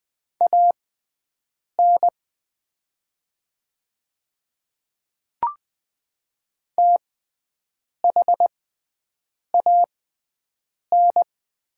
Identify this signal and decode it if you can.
Morse code, heard as a keyed tone, about 20 wpm, spaced as for 7 wpm, 700 Hz. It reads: AN THAN
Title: 6x Character Spacing - Top 500 Words - 20wpm